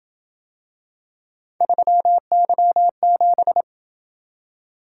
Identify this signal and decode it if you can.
Morse code, heard as a keyed tone, about 27 wpm, 700 Hz. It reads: 3Y7